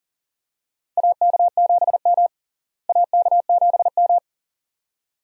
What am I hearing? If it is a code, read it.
Morse code, heard as a keyed tone, about 40 wpm, 700 Hz. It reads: AK7M AK7M
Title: USA Call Signs - 4 Characters - Repeated 2x - 40wpm